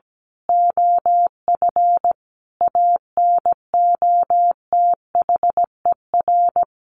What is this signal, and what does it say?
Morse code, heard as a keyed tone, about 17 wpm, 700 Hz. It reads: OF ANOTHER
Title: A Sound Of Thunder 17wpm